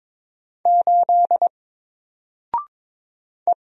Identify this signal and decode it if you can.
Morse code, heard as a keyed tone, about 22 wpm, 700 Hz. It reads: 8 E
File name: Single-Letter-Number-22wpm